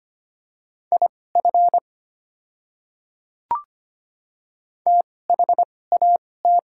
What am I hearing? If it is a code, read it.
Morse code, heard as a keyed tone, about 25 wpm, spaced as for 18 wpm, 700 Hz. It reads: IF THAT
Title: 2x Character Spacing - Top 100 Words - 25wpm